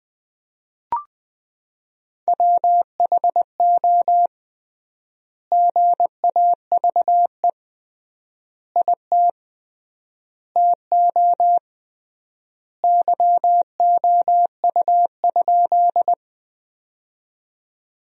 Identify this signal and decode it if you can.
Morse code, heard as a keyed tone, about 20 wpm, 700 Hz. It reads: WHO GAVE IT TO YOU?
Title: 3x Word-Spacing - Sentences from Top 500 Words - 20wpm